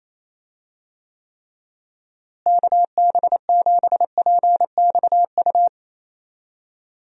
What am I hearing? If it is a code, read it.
Morse code, heard as a keyed tone, about 28 wpm, 700 Hz. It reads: KB7PXU
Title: USA Call Signs - 6 Characters - Repeated 3x - 28wpm